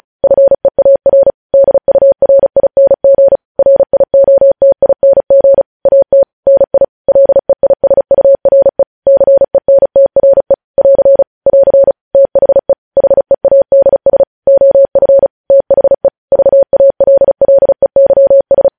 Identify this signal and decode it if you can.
Morse code, heard as a keyed tone, about 36 wpm, 550 Hz. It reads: FEAR DURING RIOTING AT NI LEISURE CENTRE <AR> <AR> THE HEADS OF THE VALLEYS
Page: news headlines in morse code at 35 wpm